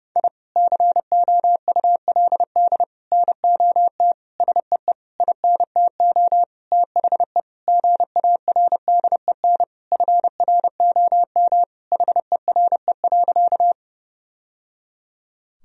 Morse code, heard as a keyed tone, about 30 wpm, 700 Hz. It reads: I COULD NOT SEE INTO THE GARDEN FROM HERE.
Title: Sentences from Top 1000 Words - Review Part 4 - 30wpm